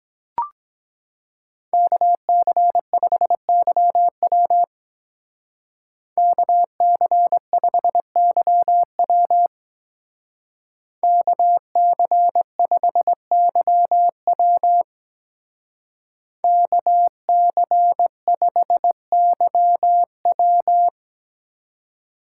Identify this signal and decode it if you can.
Morse code, heard as a keyed tone, about 22 wpm, 700 Hz. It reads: KC5YW KC5YW KC5YW KC5YW